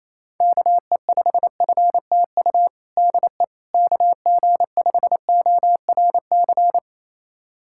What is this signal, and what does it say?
Morse code, heard as a keyed tone, about 28 wpm, 700 Hz. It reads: KE5FTU DE KG5ORC
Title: USA Callsign DE Callsign 28wpm